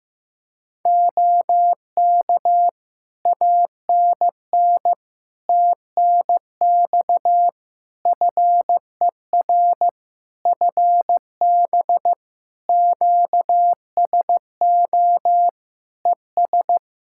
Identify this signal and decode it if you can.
Morse code, heard as a keyed tone, about 15 wpm, 700 Hz. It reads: OK ANN TNX FER FB QSO ES
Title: Practice QSO - Exchange 4 - Ending - 15wpm